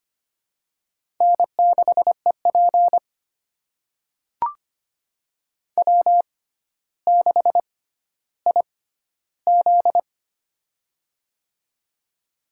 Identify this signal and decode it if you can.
Morse code, heard as a keyed tone, about 25 wpm, spaced as for 13 wpm, 700 Hz. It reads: N6EP W6IZ
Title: USA Call Signs - 4 Characters - 6x Character Spacing - 25wpm